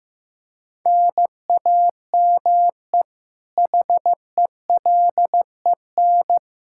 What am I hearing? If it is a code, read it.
Morse code, heard as a keyed tone, about 16 wpm, 700 Hz. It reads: NAME HELEN
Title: Practice QSO - Exchange 1 - Beginning - 15wpm